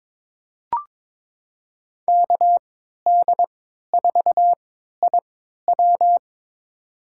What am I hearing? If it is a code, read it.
Morse code, heard as a keyed tone, about 22 wpm, spaced as for 12 wpm, 700 Hz. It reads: KD4IW
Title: USA Call Signs - 5 Characters - 3x Character Spacing - 22wpm